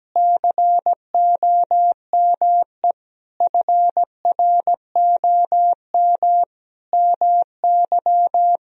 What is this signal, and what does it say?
Morse code, heard as a keyed tone, about 17 wpm, 700 Hz. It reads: COME FROM MY
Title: Sentences from Top 1000 Words - Review Part 4 - 17wpm